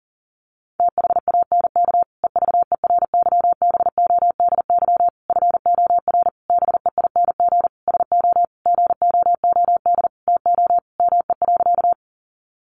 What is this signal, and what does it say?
Morse code, heard as a keyed tone, about 40 wpm, 700 Hz. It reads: THANK EVERYBODY FOR BEING SO GOOD TO ME.
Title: Sentences from Top 1400 Words 40wpm